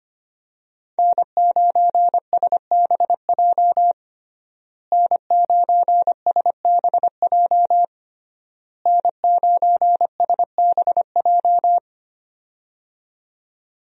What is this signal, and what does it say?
Morse code, heard as a keyed tone, about 25 wpm, 700 Hz. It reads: N9SBJ N9SBJ N9SBJ
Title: USA Call Signs - 5 Characters - Repeated 3x - 25wpm